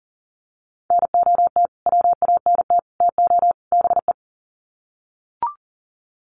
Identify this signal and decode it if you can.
Morse code, heard as a keyed tone, about 37 wpm, 700 Hz. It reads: NOT WANT TO BE
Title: Sets of 4 Words 40wpm